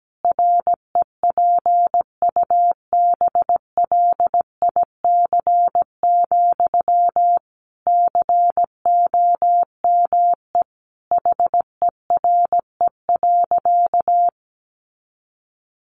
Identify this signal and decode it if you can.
Morse code, heard as a keyed tone, about 17 wpm, 700 Hz. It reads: REPUBLIC, COME HERE.
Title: Sentences from Top 1500 Words 17wpm